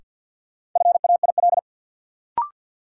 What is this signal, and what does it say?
Morse code, heard as a keyed tone, about 47 wpm, 700 Hz. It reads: FAIL
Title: Sentences from Top 1100 Words - New Word Drill Alt - 50wpm